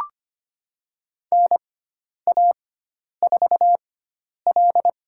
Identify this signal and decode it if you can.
Morse code, heard as a keyed tone, about 25 wpm, spaced as for 10 wpm, 700 Hz. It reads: NA4L
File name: Lesson 16g - Letter c - Call Signs 25wpm